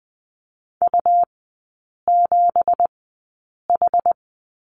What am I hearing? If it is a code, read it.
Morse code, heard as a keyed tone, about 20 wpm, spaced as for 8 wpm, 700 Hz. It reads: U7H
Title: 166 Lesson 27e - Number 7 - Three Characters 20wpm